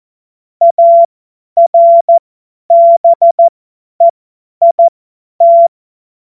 Morse code, heard as a keyed tone, about 14 wpm, spaced as for 10 wpm, 680 Hz. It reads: ARBEIT
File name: Witz-21-Oktober-2025-50-F70